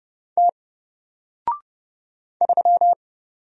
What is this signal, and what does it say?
Morse code, heard as a keyed tone, about 30 wpm, 700 Hz. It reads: T 3
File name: Single-Letter-Number-30wpm